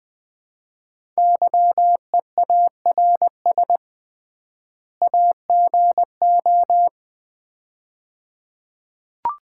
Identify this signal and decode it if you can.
Morse code, heard as a keyed tone, about 20 wpm, 690 Hz. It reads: YEARS AGO
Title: 3x Word-Spacing - Sets of 2 Words - 20wpm